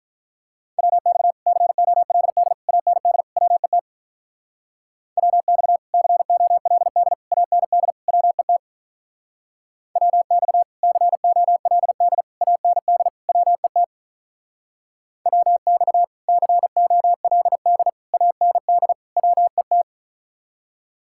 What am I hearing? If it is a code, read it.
Morse code, heard as a keyed tone, about 44 wpm, 700 Hz. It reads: WX COLD AND WET WX COLD AND WET WX COLD AND WET WX COLD AND WET